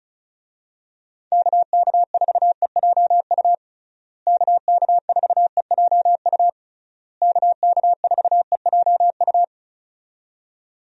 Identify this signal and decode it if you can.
Morse code, heard as a keyed tone, about 34 wpm, 700 Hz. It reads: KK4EJU KK4EJU KK4EJU
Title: USA Call Signs - 6 Characters - Repeated 3x - 35wpm